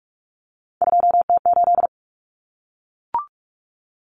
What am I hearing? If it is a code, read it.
Morse code, heard as a keyed tone, about 44 wpm, 700 Hz. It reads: 2T8
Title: Common Cut Numbers - 3-Digit 45wpm